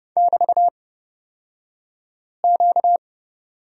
Morse code, heard as a keyed tone, about 30 wpm, spaced as for 5 wpm, 700 Hz. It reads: <BT>Q
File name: Lesson 38d - Letter q - Two Characters 30wpm